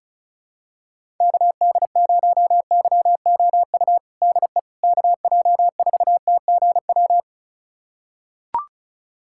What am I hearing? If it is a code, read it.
Morse code, heard as a keyed tone, about 34 wpm, 700 Hz. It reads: KD0YOU DE KJ4TGW
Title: USA Callsign DE Callsign - 6 Characters - 35wpm